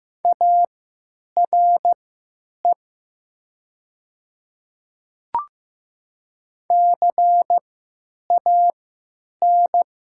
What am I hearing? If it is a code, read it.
Morse code, heard as a keyed tone, about 16 wpm, spaced as for 9 wpm, 700 Hz. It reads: ARE CAN
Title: 3x Character Spacing - Top 500 Words - 15wpm